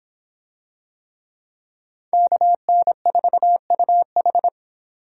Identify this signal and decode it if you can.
Morse code, heard as a keyed tone, about 26 wpm, 700 Hz. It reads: KN4UH